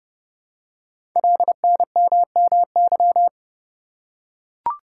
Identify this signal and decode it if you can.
Morse code, heard as a keyed tone, about 30 wpm, 700 Hz. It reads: LNMMY
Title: Code Groups - 5 Letters - 30wpm